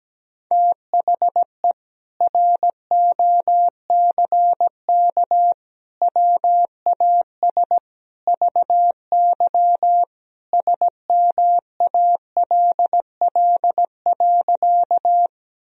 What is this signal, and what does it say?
Morse code, heard as a keyed tone, about 17 wpm, 700 Hz. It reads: THE ROCK WAS VY SMALL.
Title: Sentences from Top 400 Words and Abbreviations 17wpm